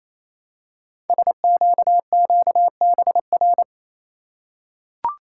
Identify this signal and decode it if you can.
Morse code, heard as a keyed tone, about 29 wpm, 700 Hz. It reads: SQQBR
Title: Code Groups - 5 Letters - 28wpm